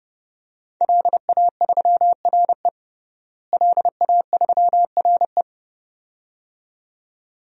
Morse code, heard as a keyed tone, about 32 wpm, 700 Hz. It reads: LA3RE LA3RE
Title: DXCC-100 Call Signs - 5 Characters - Repeated 2x - 30wpm